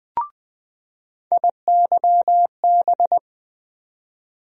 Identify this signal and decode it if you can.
Morse code, heard as a keyed tone, about 20 wpm, 700 Hz. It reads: IYB
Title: USA Call Sign Postfixes - 3 Letters 20wpm